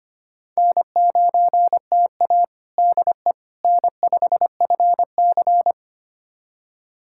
Morse code, heard as a keyed tone, about 25 wpm, 700 Hz. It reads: N9TA DE N5FC
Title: USA Callsign DE Callsign - 4 Characters - 25wpm